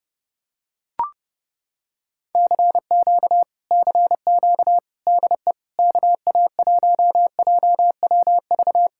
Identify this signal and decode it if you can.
Morse code, heard as a keyed tone, about 30 wpm, 700 Hz. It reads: CQ CQ DE KA1JWV